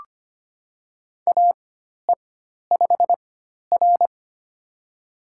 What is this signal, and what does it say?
Morse code, heard as a keyed tone, about 25 wpm, spaced as for 12 wpm, 700 Hz. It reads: AE5R
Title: USA Call Signs - 4 Characters - 4x Character Spacing - 25wpm